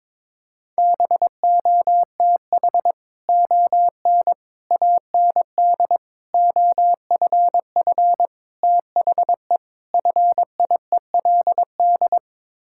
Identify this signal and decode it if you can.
Morse code, heard as a keyed tone, about 22 wpm, 700 Hz. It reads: BOTH ON AND OFF THE FIELD
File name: practice / 22wpm / Sets of 6 Words - 22wpm